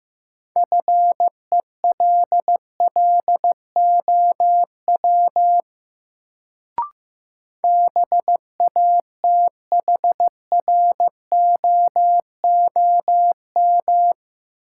Morse code, heard as a keyed tone, about 15 wpm, 700 Hz. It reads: FELLOW BATHROOM
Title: Sets of 5 Words Encore - New Word Drill - 15wpm